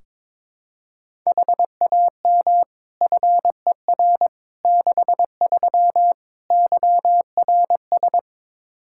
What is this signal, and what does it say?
Morse code, heard as a keyed tone, about 22 wpm, 700 Hz. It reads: HAM FER 63 YRS
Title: Practice QSO - Exchange 3 - More Info - 22wpm